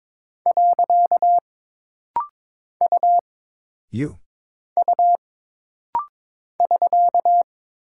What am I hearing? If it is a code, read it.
Morse code, heard as a keyed tone, about 22 wpm, 700 Hz. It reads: . U U <SK>
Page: This rapid-fire timing is geared to push your ICR (Instant Character Recognition) skills.